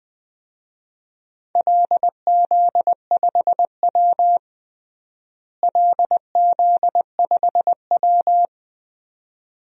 Morse code, heard as a keyed tone, about 21 wpm, 700 Hz. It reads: LZ5W LZ5W
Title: DXCC-100 Call Signs - 4 Characters - Repeated 2x - 20wpm